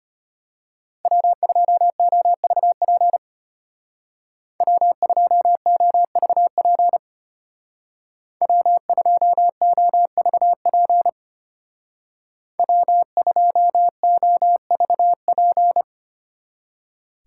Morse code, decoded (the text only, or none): W2OVP W2OVP W2OVP W2OVP